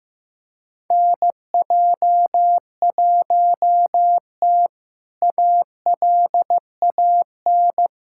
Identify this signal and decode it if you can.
Morse code, heard as a keyed tone, about 15 wpm, 700 Hz. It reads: NJ1T ALAN
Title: K1USN SST QSO Practice - 15wpm